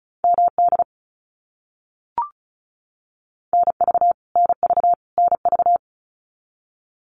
Morse code, heard as a keyed tone, about 34 wpm, 700 Hz. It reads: MD NV NV NV
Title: US States Repeated 3x 35wpm